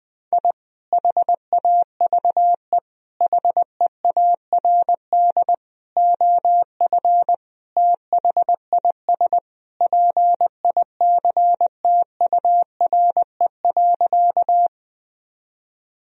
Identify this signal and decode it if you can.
Morse code, heard as a keyed tone, about 21 wpm, 700 Hz. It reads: I HAVE HEARD OF THIS PICTURE.
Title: Sentences from Top 1000 Words - Review Part 1 - 20wpm